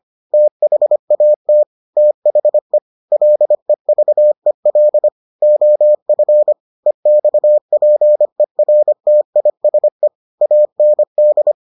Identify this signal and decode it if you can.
Morse code, heard as a keyed tone, about 25 wpm, 600 Hz. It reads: THAT THE LEVEL OF EXPERTISE AND